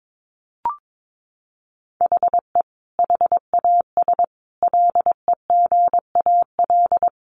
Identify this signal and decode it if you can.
Morse code, heard as a keyed tone, about 22 wpm, 700 Hz. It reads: HE HAS LEGAL
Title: Sentences from Top 1200 Words 22wpm